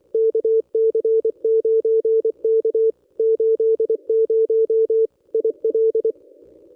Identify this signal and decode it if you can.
Morse code, heard as a keyed tone, about 24 wpm, 450 Hz. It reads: KC9K 80 IL
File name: contests / ss-cw-k5tr-op-n5rz -1200